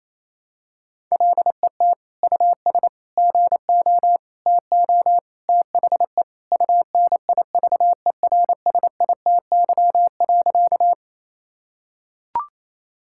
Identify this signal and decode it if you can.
Morse code, heard as a keyed tone, about 27 wpm, 700 Hz. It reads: LET US GO TO THE UNIVERSITY.
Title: Sentences from Top 1000 Words - Review Part 5 - 28wpm